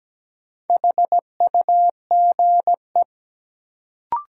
Morse code, heard as a keyed tone, about 17 wpm, 700 Hz. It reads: HUGE
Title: Sentences from Top 1100 Words - New Word Drill Alt - 17wpm